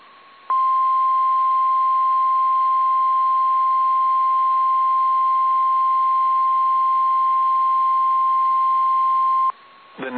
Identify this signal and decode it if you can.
Emergency Alert System (SAME) tones (the 1050 Hz attention tone). It is heard from 0.5 to 9.6 s.
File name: eas / KSDZKDJL / KSDZKDJL